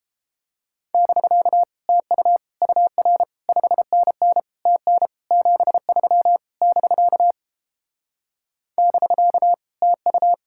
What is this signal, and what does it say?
Morse code, heard as a keyed tone, about 29 wpm, 700 Hz. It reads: <BK> TU UR 5NN TN 73 <BK> <BK> TU